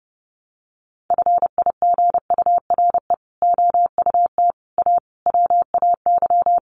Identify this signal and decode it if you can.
Morse code, heard as a keyed tone, about 30 wpm, 700 Hz. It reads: FIGURE OUT A WAY
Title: Sets of 5 Words Encore2 30wpm